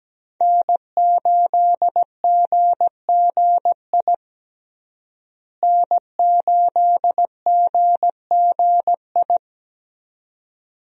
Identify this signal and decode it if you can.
Morse code, heard as a keyed tone, about 17 wpm, 700 Hz. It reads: N8GGI N8GGI
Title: USA Call Signs - 5 Characters - Repeated 2x - 17wpm